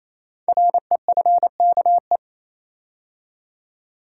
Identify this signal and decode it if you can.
Morse code, heard as a keyed tone, about 29 wpm, 700 Hz. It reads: REFKE
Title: Code Groups - 5 Letter:Numbers - 28wpm